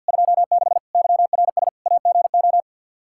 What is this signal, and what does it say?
Morse code, heard as a keyed tone, about 49 wpm, 700 Hz. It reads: 26 YRS AGO
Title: Practice QSO - Exchange 3 - More Info - 50wpm